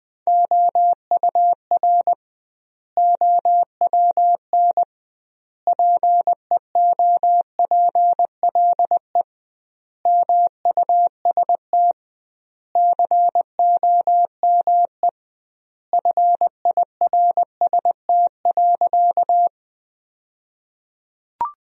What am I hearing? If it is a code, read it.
Morse code, heard as a keyed tone, about 21 wpm, 700 Hz. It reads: OUR OWN PEOPLE MUST COME FIRST.
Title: 2x Word-Spacing - Sentences from Top 300 Words - 20wpm